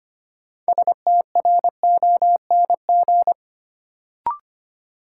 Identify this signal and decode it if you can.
Morse code, heard as a keyed tone, about 25 wpm, 700 Hz. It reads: STRONG
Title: Sentences from Top 500 Words - New Word Drill - 25wpm